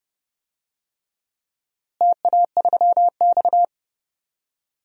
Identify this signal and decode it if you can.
Morse code, heard as a keyed tone, about 29 wpm, 700 Hz. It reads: TA3X
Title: DXCC-100 Call Signs - 4 Characters - Repeated 3x - 30wpm